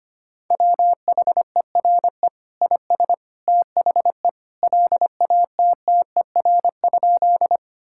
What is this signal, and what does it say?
Morse code, heard as a keyed tone, about 25 wpm, 700 Hz. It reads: WHERE IS THE LATTER?
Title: Sentences from Top 1600 Words 25wpm